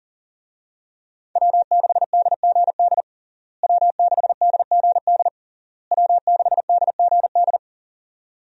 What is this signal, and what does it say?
Morse code, heard as a keyed tone, about 40 wpm, 700 Hz. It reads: W6DGD W6DGD W6DGD
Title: USA Call Signs - 5 Characters - Repeated 3x - 40wpm